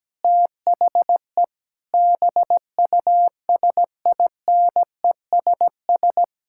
Morse code, heard as a keyed tone, about 17 wpm, 700 Hz. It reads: THE BUSINESS
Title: Sentences from Top 1000 Words - Review Part 6 - 17wpm